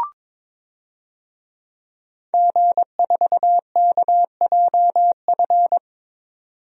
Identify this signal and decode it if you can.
Morse code, heard as a keyed tone, about 22 wpm, 700 Hz. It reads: G4KJF
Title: DXCC-100 Call Signs - 5 Characters - 22wpm